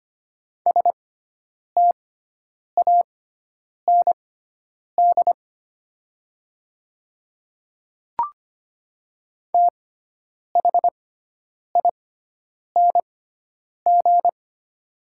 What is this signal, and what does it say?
Morse code, heard as a keyed tone, about 25 wpm, spaced as for 9 wpm, 700 Hz. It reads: STAND THING
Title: 6x Character Spacing - Top 500 Words - 25wpm